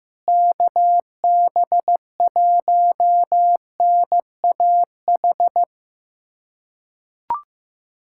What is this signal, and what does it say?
Morse code, heard as a keyed tone, about 15 wpm, 700 Hz. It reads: KB1NAH
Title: USA Callsign DE Callsign 15wpm